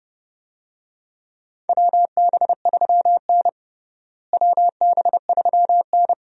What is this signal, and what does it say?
Morse code, heard as a keyed tone, about 32 wpm, 700 Hz. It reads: WB3N WB3N
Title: USA Call Signs - 4 Characters - Repeated 2x - 30wpm